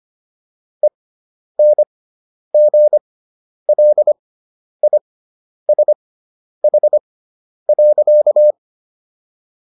Morse code, heard as a keyed tone, about 25 wpm, spaced as for 10 wpm, 600 Hz. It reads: ENGLISH.